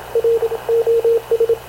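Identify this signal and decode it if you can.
Morse code, heard as a keyed tone, about 26 wpm, 460 Hz, with radio noise. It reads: R OS
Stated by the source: Havana Radio / CLA sending Spanish press (PX) in Morse, recorded by Radio Officer Paul Dumesnil onboard CNAV QUEST/CZDO.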